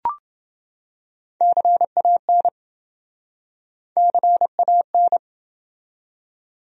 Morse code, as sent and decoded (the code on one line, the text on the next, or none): -.-. .- -. / -.-. .- -.
CAN CAN